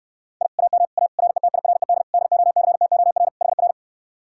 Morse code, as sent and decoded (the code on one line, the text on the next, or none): .. / .- -- / .- / .-. - .. .-. -.. / -.. --- -.-. - --- .-. / .... .-.
I AM A RETIRED DOCTOR HR